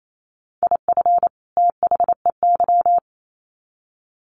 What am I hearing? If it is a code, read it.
Morse code, heard as a keyed tone, about 28 wpm, 700 Hz. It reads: IF THEY
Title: Sets of 2 Words 28wpm